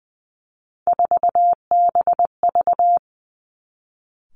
Morse code, heard as a keyed tone, about 20 wpm, 700 Hz. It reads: 4BV
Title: Three Challenging Letter-Number Combinations 20wpm